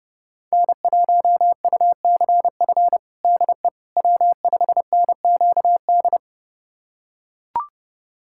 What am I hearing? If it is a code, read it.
Morse code, heard as a keyed tone, about 32 wpm, 700 Hz. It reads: N1UCF DE W5NQD
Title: USA Callsign DE Callsign - 5 Characters - 30wpm